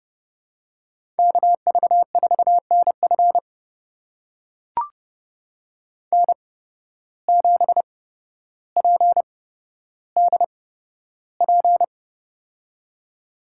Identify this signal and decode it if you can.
Morse code, heard as a keyed tone, about 30 wpm, spaced as for 13 wpm, 700 Hz. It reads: KV4NF N7PDP
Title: USA Call Signs - 5 Characters - 8x Character Spacing - 30wpm